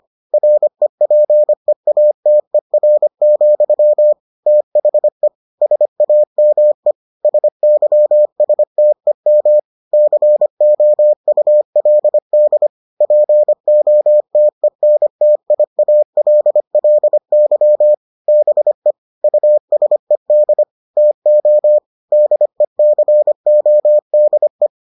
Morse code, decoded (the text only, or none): REPEATER, THE SAME SYSTEM COULD POTENTIALLY BE USED TO DECODE